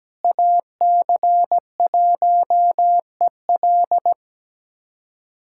Morse code, heard as a keyed tone, about 17 wpm, 700 Hz. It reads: AC1EL
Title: USA Callsign DE Callsign 17wpm